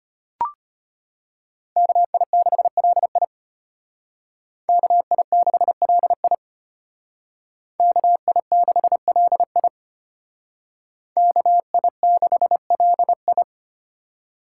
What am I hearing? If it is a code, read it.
Morse code, heard as a keyed tone, about 32 wpm, 700 Hz. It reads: KI6LI KI6LI KI6LI KI6LI